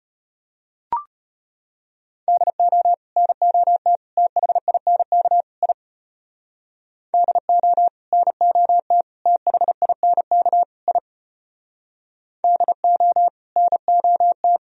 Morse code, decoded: DO NOT THINK I DO NOT THINK I DO NOT

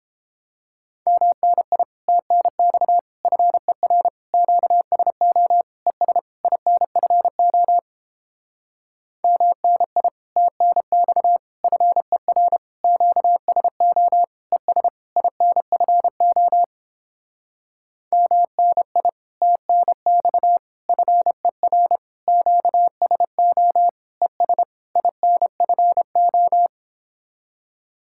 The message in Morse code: MNI TNX FER QSO ES INFO MNI TNX FER QSO ES INFO MNI TNX FER QSO ES INFO

